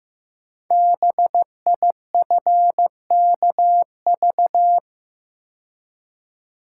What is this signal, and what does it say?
Morse code, heard as a keyed tone, about 15 wpm, 700 Hz. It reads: BIFKV
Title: Code Groups - 5 Letter:Numbers - 15wpm